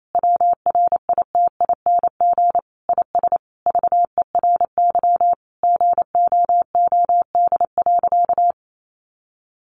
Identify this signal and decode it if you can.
Morse code, heard as a keyed tone, about 28 wpm, 700 Hz. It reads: WRITING IS VERY GOOD.
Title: Sentences from Top 800 Words 28wpm